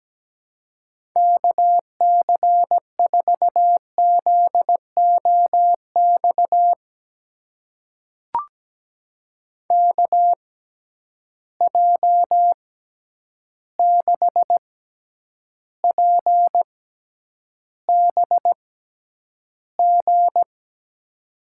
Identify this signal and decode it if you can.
Morse code, heard as a keyed tone, about 17 wpm, spaced as for 9 wpm, 700 Hz. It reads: KC4ZOX KJ6PBG
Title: USA Call Signs - 6 Characters - 6x Character Spacing - 17wpm